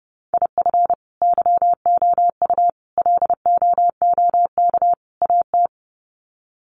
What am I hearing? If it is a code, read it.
Morse code, heard as a keyed tone, about 30 wpm, 700 Hz. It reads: IF YOU LOOK AT
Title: Sets of 4 Words 30wpm